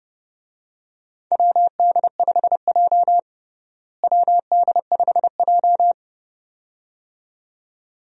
Morse code, heard as a keyed tone, about 29 wpm, 700 Hz. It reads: WD5J WD5J
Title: USA Call Signs - 4 Characters - Repeated 2x - 30wpm